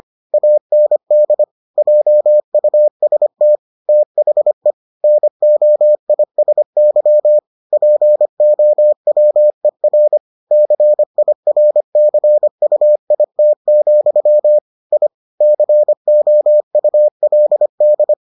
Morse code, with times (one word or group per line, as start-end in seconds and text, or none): 0.3-1.4 AND
1.8-3.6 JUST
3.9-4.7 THE
5.0-7.4 NOISY
7.7-10.2 POWER
10.5-14.6 CIRCUIT,
14.9-15.1 I
15.4-18.1 COULD